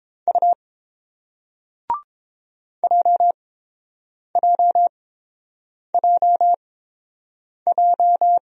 Morse code, heard as a keyed tone, about 30 wpm, spaced as for 7 wpm, 700 Hz. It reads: U JJJJ